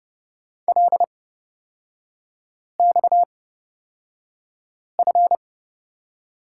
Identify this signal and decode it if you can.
Morse code, heard as a keyed tone, about 30 wpm, spaced as for 5 wpm, 690 Hz. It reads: LXF
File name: Lesson 37e - Letter x - Three Characters 30wpm